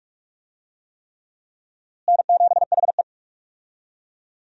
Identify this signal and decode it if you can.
Morse code, heard as a keyed tone, about 46 wpm, 700 Hz. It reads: N7HE
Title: USA Call Signs - 4 Characters - Repeated 3x - 45wpm